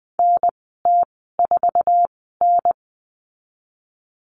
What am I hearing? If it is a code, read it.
Morse code, heard as a keyed tone, about 20 wpm, spaced as for 15 wpm, 700 Hz. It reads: NT4N
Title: 048 Lesson 9g - Number 4 - Call Signs 20wpm